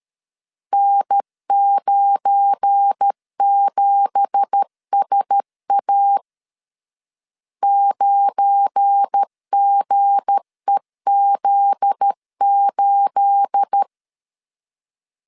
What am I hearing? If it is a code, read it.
Morse code, heard as a keyed tone, about 13 wpm, 800 Hz. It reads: N97SA 9GEZ8
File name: CW / DET / 1 9 1500C S C 12wpm 3811